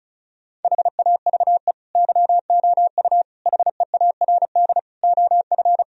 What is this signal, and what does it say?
Morse code, heard as a keyed tone, about 36 wpm, 700 Hz. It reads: HAVE YOU HEARD OF
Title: Sentences from Top 1000 Words - Review Part 4 - 35wpm